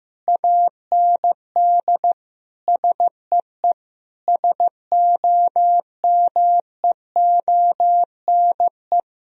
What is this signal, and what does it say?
Morse code, heard as a keyed tone, about 16 wpm, 700 Hz. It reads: AND SEE SOMEONE
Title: Sentences from Top 1000 Words - Review Part 4 - 15wpm